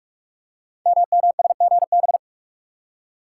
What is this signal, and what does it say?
Morse code, heard as a keyed tone, about 44 wpm, 700 Hz. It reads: MMSGB
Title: Code Groups - 5 Letters - 45wpm